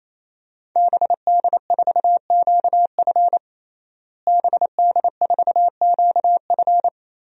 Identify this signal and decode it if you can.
Morse code, heard as a keyed tone, about 28 wpm, 700 Hz. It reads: BD4QF BD4QF
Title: DXCC-100 Call Signs - 5 Characters - Repeated 3x - 28wpm